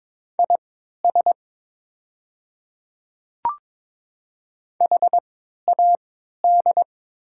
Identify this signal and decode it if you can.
Morse code, heard as a keyed tone, about 22 wpm, spaced as for 12 wpm, 700 Hz. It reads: IS HAD
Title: 3x Character Spacing - Top 200 Words - 22wpm